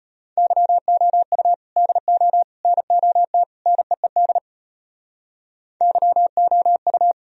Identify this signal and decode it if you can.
Morse code, heard as a keyed tone, about 34 wpm, 700 Hz. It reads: YOU DO NOT NEED YOU